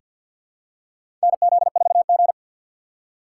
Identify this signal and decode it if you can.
Morse code, heard as a keyed tone, about 46 wpm, 700 Hz. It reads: NZ4G
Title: USA Call Signs - 2x Character Spacing - 50wpm